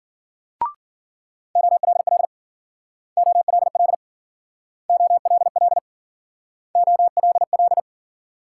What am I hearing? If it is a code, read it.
Morse code, heard as a keyed tone, about 52 wpm, 700 Hz. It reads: OLL OLL OLL OLL